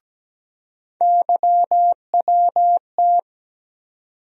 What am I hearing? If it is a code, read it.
Morse code, heard as a keyed tone, about 17 wpm, 700 Hz. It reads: YWT
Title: USA Call Sign Postfixes - 3 Letters 17wpm